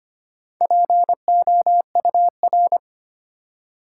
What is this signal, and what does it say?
Morse code, heard as a keyed tone, about 25 wpm, 700 Hz. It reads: POUR
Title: Four Letter Words - 25wpm